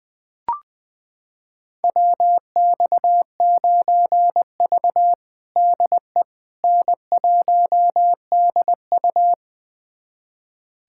Morse code, heard as a keyed tone, about 20 wpm, 700 Hz. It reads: WX9V DE N1DU
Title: USA Callsign DE Callsign - 4 Characters - 20wpm